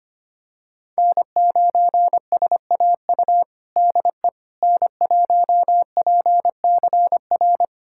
Morse code, heard as a keyed tone, about 25 wpm, 700 Hz. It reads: N9SAU DE N1PCR
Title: USA Callsign DE Callsign - 5 Characters - 25wpm